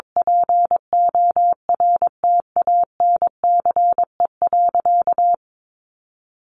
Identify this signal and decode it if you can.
Morse code, heard as a keyed tone, about 22 wpm, 700 Hz. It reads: PORTANCE.
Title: Sentences from Top 1200 Words 22wpm